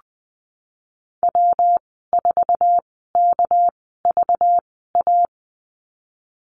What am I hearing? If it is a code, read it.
Morse code, heard as a keyed tone, about 20 wpm, spaced as for 15 wpm, 700 Hz. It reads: W4KVA
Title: 238 Lesson 38g - Letter q - Call Signs 20wpm